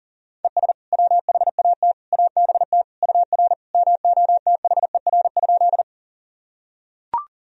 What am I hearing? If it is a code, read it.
Morse code, heard as a keyed tone, about 40 wpm, 700 Hz. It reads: ES WHAT ABT UR MOTHER?
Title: Sentences from Top 200 Words and Abbreviations 40wpm